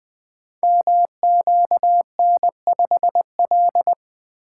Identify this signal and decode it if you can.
Morse code, heard as a keyed tone, about 20 wpm, 700 Hz. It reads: MQN5L
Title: Code Groups - 5 Letter:Numbers - 20wpm